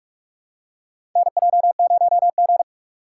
Morse code, heard as a keyed tone, about 46 wpm, 700 Hz. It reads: NJ0G
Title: USA Call Signs - 4 Characters - 3x Character Spacing - 45wpm